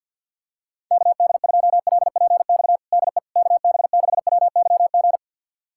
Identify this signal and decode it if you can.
Morse code, heard as a keyed tone, about 51 wpm, 700 Hz. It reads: KD2LPX DE KB6WYG
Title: USA Callsign DE Callsign - 6 Characters - 50wpm